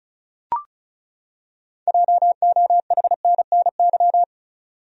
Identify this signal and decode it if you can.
Morse code, heard as a keyed tone, about 36 wpm, 700 Hz. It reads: JOHNNY
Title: USA Common Names - 35wpm